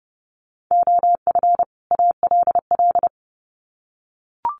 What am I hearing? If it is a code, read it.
Morse code, heard as a keyed tone, about 30 wpm, 700 Hz. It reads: OF ALL
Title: Sets of 2 Words 30wpm